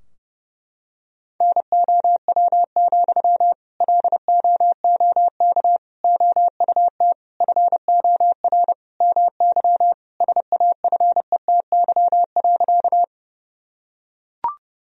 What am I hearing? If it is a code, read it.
Morse code, heard as a keyed tone, about 28 wpm, 700 Hz. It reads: NOW, LOOK OUT FOR MY SAFETY.
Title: Sentences from Top 1000 Words - Review Part 4 - 30wpm